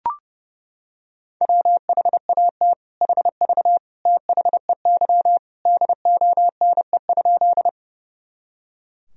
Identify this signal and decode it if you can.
Morse code, heard as a keyed tone, about 30 wpm, 700 Hz. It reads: WHAT HV THEY DONE?
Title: Sentences from Top 500 Words and Abbreviations 30wpm